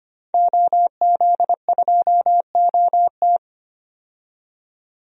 Morse code, heard as a keyed tone, about 25 wpm, 700 Hz. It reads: OZ2OT
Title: DXCC-100 Call Signs - 5 Characters - 25wpm